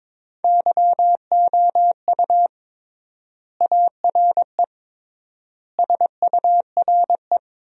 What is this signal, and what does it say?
Morse code, heard as a keyed tone, about 22 wpm, 700 Hz. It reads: YOU ARE SURE
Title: 3x Word-Spacing - Sentences from Top 300 Words - 22wpm